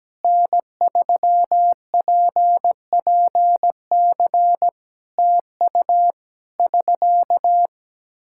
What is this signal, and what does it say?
Morse code, heard as a keyed tone, about 17 wpm, 700 Hz. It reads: N3PPC TU <SK>
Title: K1USN SST QSO Practice 17wpm